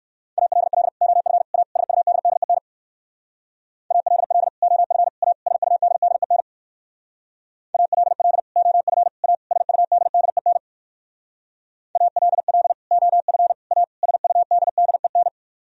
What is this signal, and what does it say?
Morse code, heard as a keyed tone, about 58 wpm, 700 Hz. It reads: ALL OF A SUDDEN ALL OF A SUDDEN ALL OF A SUDDEN ALL OF A SUDDEN